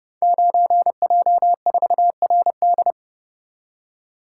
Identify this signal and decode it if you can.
Morse code, heard as a keyed tone, about 30 wpm, 700 Hz. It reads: 9J4RD
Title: Code Groups - 5 Letter:Numbers - 30wpm